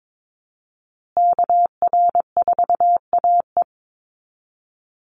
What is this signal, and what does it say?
Morse code, heard as a keyed tone, about 22 wpm, 700 Hz. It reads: KR4AE
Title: Call Signs Repeated 2x - 22wpm